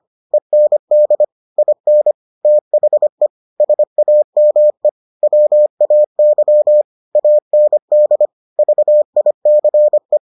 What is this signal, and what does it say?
Morse code, heard as a keyed tone, about 25 wpm, 600 Hz. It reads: END IN THE SAME WAY AND VICE